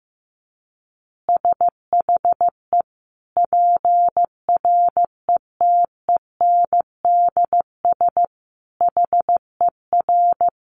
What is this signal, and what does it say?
Morse code, heard as a keyed tone, about 15 wpm, 700 Hz. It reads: SHE PRETENDS HER
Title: Additional Easy Phrases 15wpm